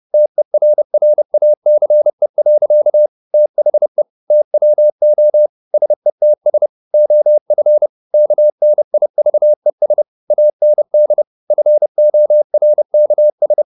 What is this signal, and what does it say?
Morse code, heard as a keyed tone, about 30 wpm, 600 Hz. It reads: TERRACE. THE TWO SETS OF KNIVES AND FORKS
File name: the old man and the sea / the old man and the sea 1 30wpm